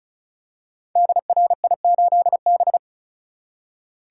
Morse code, heard as a keyed tone, about 36 wpm, 700 Hz. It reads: DRI8B
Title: Code Groups - 5 Letter:Numbers - 35wpm